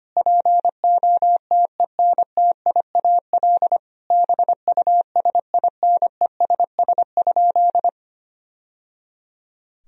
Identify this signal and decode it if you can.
Morse code, heard as a keyed tone, about 25 wpm, 700 Hz. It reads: POTENTIAL BUSINESS?
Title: Sentences from Top 1000 Words - Review Part 4 - 25wpm